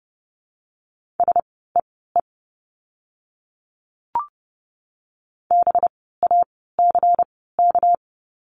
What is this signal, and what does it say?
Morse code, heard as a keyed tone, about 30 wpm, spaced as for 17 wpm, 700 Hz. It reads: SEE BACK
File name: course / 30:10wpm / 237 Lesson 38f - Letter q - Words 30wpm